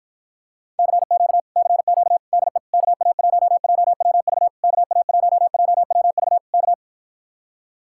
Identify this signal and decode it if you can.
Morse code, heard as a keyed tone, about 52 wpm, 700 Hz. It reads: CQ CQ DE KA1JWV KA1JWV K